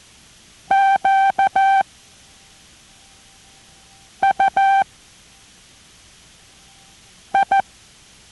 Morse code, heard as a keyed tone, about 14 wpm, 780 Hz. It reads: Q U I